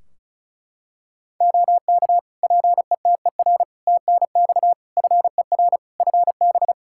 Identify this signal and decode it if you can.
Morse code, heard as a keyed tone, about 36 wpm, 700 Hz. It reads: OK PETER TNX FER FB
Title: Practice QSO - Exchange 4 - Ending - 35wpm